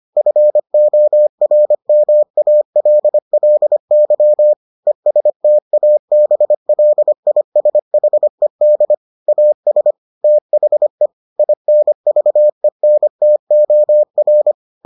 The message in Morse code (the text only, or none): FORMALLY ESTABLISHED AS THE INVENTOR